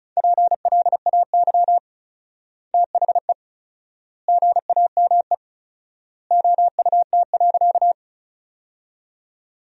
Morse code, heard as a keyed tone, about 34 wpm, 700 Hz. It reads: PLAY THE GAME OUT.
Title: 4x Word-Spacing - Sentences from Top 500 Words - 35wpm